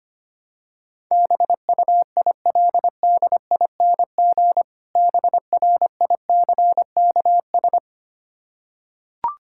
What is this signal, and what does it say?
Morse code, heard as a keyed tone, about 25 wpm, 700 Hz. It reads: BUILDING BRICKS
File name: practice / 25wpm / Difficult Words with Hints - 25wpm